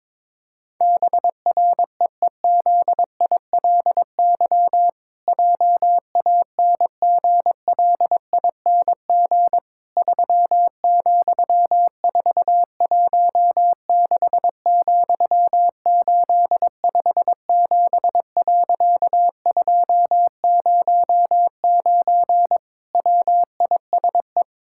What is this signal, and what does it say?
Morse code, heard as a keyed tone, about 22 wpm, 700 Hz. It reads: BREEZILY JANGLING 3,416,857.209 WISE
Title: Pangram Challenge v2 - 22wpm